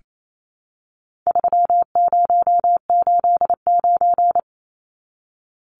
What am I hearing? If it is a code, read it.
Morse code, heard as a keyed tone, about 28 wpm, 700 Hz. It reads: 3089
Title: Numbers - 4-Digit 28wpm